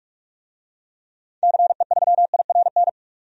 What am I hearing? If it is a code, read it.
Morse code, heard as a keyed tone, about 46 wpm, 700 Hz. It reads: CE3IRN
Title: DXCC-100 Call Signs - 6 Characters - 45wpm